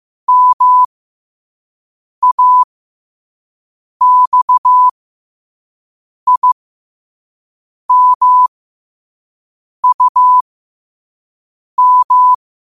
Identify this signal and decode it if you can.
Morse code, heard as a keyed tone, about 15 wpm, spaced as for 5 wpm, 1000 Hz. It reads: MAXIMUM